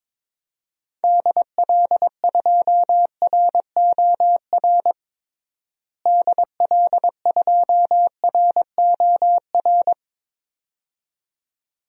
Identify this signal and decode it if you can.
Morse code, heard as a keyed tone, about 22 wpm, 700 Hz. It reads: DL2ROR DL2ROR
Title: DXCC-100 Call Signs - 6 Characters - Repeated 2x - 22wpm